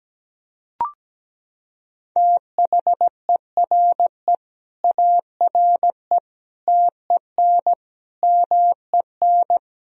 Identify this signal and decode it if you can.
Morse code, heard as a keyed tone, about 17 wpm, 700 Hz. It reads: THERE ARE TEN MEN
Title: Sentences from Top 1000 Words - Review Part 2 - 17wpm